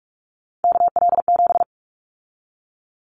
Morse code, heard as a keyed tone, about 44 wpm, 700 Hz. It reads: KL7
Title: Callsign Prefixes for Top 100 DXCC - 45wpm